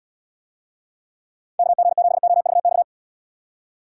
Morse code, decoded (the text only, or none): KG7OVX